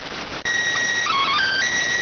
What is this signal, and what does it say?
SSTV (slow-scan television), PD 290.